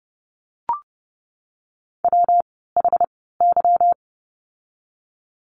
Morse code, heard as a keyed tone, about 30 wpm, spaced as for 17 wpm, 700 Hz. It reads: WHY
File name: course / 30:10wpm / 230 Lesson 37f - Letter x - Words 30wpm